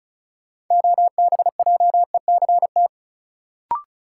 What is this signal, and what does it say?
Morse code, heard as a keyed tone, about 34 wpm, 700 Hz. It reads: OBJECT
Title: Sentences from Top 500 Words - New Word Drill - 35wpm